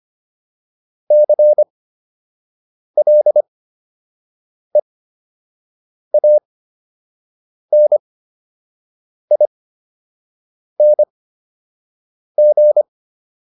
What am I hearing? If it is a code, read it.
Morse code, heard as a keyed tone, about 25 wpm, spaced as for 6 wpm, 600 Hz. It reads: CLEANING